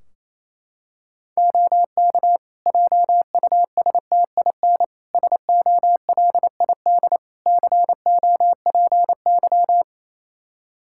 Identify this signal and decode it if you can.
Morse code, heard as a keyed tone, about 28 wpm, 700 Hz. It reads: OK JUSTIN SOLID COPY
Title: Practice QSO - Exchange 3 - More Info - 28wpm